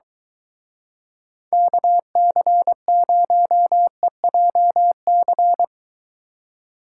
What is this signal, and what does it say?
Morse code, heard as a keyed tone, about 23 wpm, 700 Hz. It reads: KC0EJC